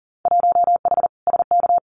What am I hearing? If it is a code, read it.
Morse code, heard as a keyed tone, about 40 wpm, 700 Hz. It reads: 1H SK
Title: Winter Field Day Exchanges 40wpm